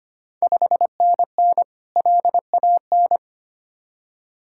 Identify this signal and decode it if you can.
Morse code, heard as a keyed tone, about 25 wpm, 700 Hz. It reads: 5NN LAN
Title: Ontario QSO Party Exchange 25wpm